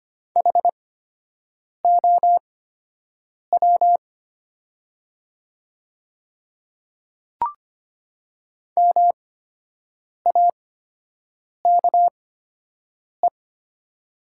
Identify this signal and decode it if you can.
Morse code, heard as a keyed tone, about 25 wpm, spaced as for 7 wpm, 700 Hz. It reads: HOW MAKE
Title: 8x Character Spacing - Top 100 Words - 25wpm